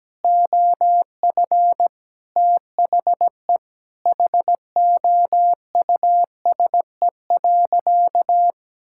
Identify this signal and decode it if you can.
Morse code, heard as a keyed tone, about 17 wpm, 700 Hz. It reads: OF THE HOUSE.
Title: Sentences from Top 1600 Words 17wpm